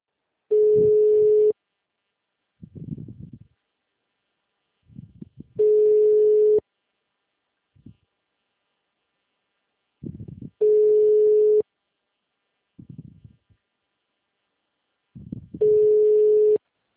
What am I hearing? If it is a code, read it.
Ringback tone (Europe, 425 Hz), with speech.
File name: out- -2011- - - .85826